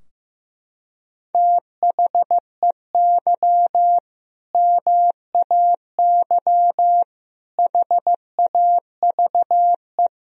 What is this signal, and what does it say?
Morse code, heard as a keyed tone, about 15 wpm, 700 Hz. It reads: THEY MAY HAVE
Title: Sentences from Top 1000 Words - Review Part 2 - 15wpm